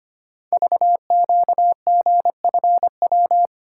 Morse code, heard as a keyed tone, about 25 wpm, 700 Hz. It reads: VQGFW
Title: Code Groups - 5 Letters - 25wpm